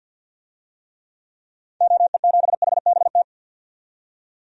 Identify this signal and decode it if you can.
Morse code, heard as a keyed tone, about 46 wpm, 700 Hz. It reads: OE7HBT
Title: DXCC-100 Call Signs - 6 Characters - 50wpm